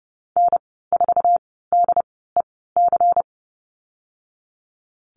Morse code, heard as a keyed tone, about 30 wpm, spaced as for 17 wpm, 700 Hz. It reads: N4DEC
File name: Lesson 37g - Letter x - Call Signs 30wpm